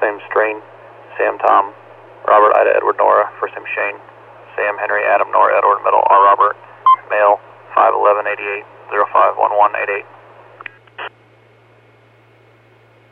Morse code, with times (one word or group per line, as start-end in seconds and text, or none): none